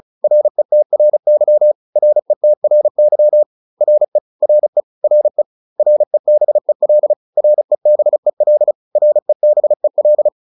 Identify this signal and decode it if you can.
Morse code, heard as a keyed tone, about 34 wpm, 600 Hz. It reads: RETRY RETRY RE RE RE REBEL REBEL REBEL